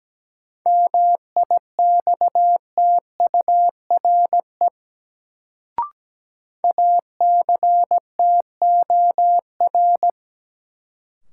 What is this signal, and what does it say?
Morse code, heard as a keyed tone, about 17 wpm, 700 Hz. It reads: MIXTURE ACTOR
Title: Sentences from Top 1500 Words - New Word Drill - 17wpm